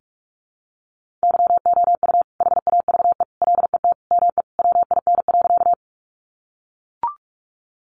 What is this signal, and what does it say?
Morse code, heard as a keyed tone, about 44 wpm, 700 Hz. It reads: YOU HAVE LET ME WIN.
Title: Sentences from Top 1500 Words 45wpm